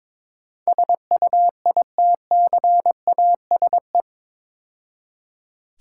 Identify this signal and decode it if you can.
Morse code, heard as a keyed tone, about 27 wpm, 700 Hz. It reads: SUITCASE
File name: Easy Words - Speed-Racing with Hints - 22wpm